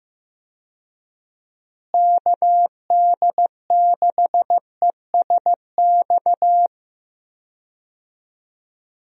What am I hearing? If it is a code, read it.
Morse code, heard as a keyed tone, about 15 wpm, 700 Hz. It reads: KD6ESX
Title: USA Call Signs - 6 Characters - Repeated 2x - 15wpm